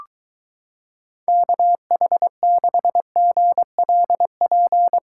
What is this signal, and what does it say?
Morse code, heard as a keyed tone, about 23 wpm, 700 Hz. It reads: KH6GLP